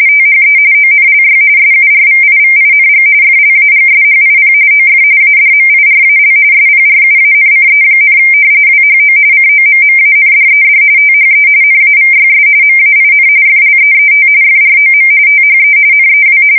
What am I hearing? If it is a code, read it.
RTTY, about 45.45 baud. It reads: THE ONLY SURVIVING FLAGSHIP OF ADMIRAL SPRUANCE, WHO COMMANDED THE NAVYS MAIN FLEET FROM THE NEW
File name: fsk170 / battleshipnj